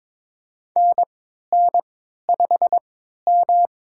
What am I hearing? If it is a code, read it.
Morse code, heard as a keyed tone, about 21 wpm, spaced as for 12 wpm, 700 Hz. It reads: NN5M
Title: USA Call Signs - 4 Characters - 3x Character Spacing - 22wpm